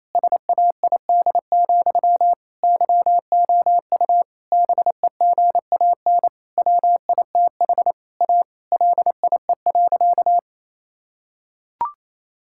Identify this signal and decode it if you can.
Morse code, heard as a keyed tone, about 27 wpm, 700 Hz. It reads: SAID, YOU BEGAN WITH A LIE.
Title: Sentences from Top 1600 Words 28wpm